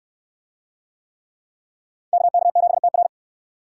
Morse code, heard as a keyed tone, about 67 wpm, 700 Hz. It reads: KK7TR